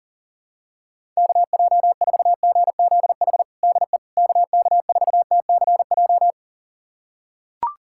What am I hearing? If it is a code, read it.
Morse code, heard as a keyed tone, about 40 wpm, 700 Hz. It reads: KJ4GZH DE KK4TCJ
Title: USA Callsign DE Callsign - 6 Characters - 40wpm